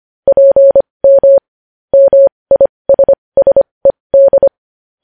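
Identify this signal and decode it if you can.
Morse code, heard as a keyed tone, about 25 wpm, spaced as for 20 wpm, 550 Hz. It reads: PM MISSED
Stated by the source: Morse practice sending news headlines at 20 wpm.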